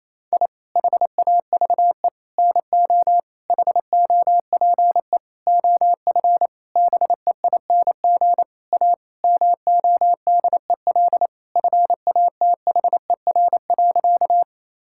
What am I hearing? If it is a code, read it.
Morse code, heard as a keyed tone, about 28 wpm, 700 Hz. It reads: I HAVE NO HOPE OF BEING A MODEL FATHER.
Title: Sentences from Top 1000 Words - Review Part 5 - 28wpm